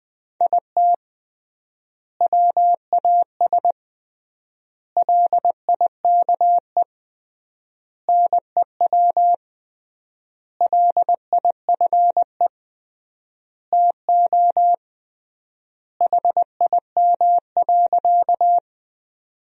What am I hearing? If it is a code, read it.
Morse code, heard as a keyed tone, about 20 wpm, 700 Hz. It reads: IT WAS LIKE NEW LIFE TO HIM.
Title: 3x Word-Spacing - Sentences from Top 300 Words - 20wpm